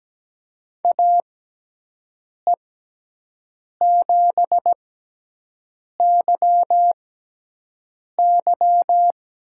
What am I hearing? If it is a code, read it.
Morse code, heard as a keyed tone, about 17 wpm, spaced as for 6 wpm, 700 Hz. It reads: AE7YY
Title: USA Call Signs - 6x Character Spacing - 17wpm